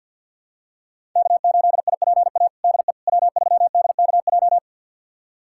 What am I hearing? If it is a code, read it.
Morse code, heard as a keyed tone, about 51 wpm, 700 Hz. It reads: K8IPA DE W3DKJ
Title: USA Callsign DE Callsign - 5 Characters - 50wpm